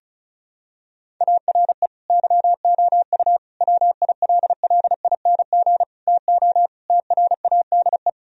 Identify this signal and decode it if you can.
Morse code, heard as a keyed tone, about 36 wpm, 700 Hz. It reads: ARE YOU WILLING TO TRADE
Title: Sentences from Top 1000 Words - Review Part 4 - 35wpm